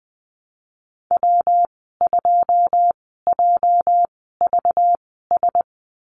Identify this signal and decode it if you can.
Morse code, heard as a keyed tone, about 20 wpm, spaced as for 14 wpm, 700 Hz. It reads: W2JVS
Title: 245 Lesson 39g - Letter z - Call Signs 20wpm